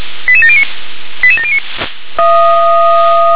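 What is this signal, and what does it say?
Five-tone selective call (ZVEI), heard as quick runs of five tones, tones of about 70 ms. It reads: 88709 88709
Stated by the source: Five-tone ZVEI selective call received on an SDR.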